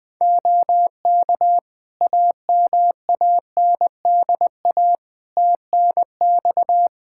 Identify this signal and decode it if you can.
Morse code, heard as a keyed tone, about 21 wpm, 700 Hz. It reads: OK AMANDA TNX
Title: K1USN SST QSO Practice 20wpm